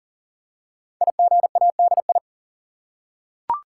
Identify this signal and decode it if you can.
Morse code, heard as a keyed tone, about 39 wpm, 700 Hz. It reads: IGADI
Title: Code Groups - 5 Letters - 40wpm